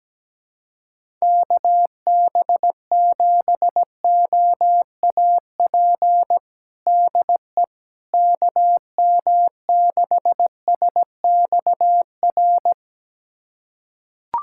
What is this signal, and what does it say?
Morse code, heard as a keyed tone, about 17 wpm, 700 Hz. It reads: KB7OAP DE KM6SXR
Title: USA Callsign DE Callsign - 6 Characters - 17wpm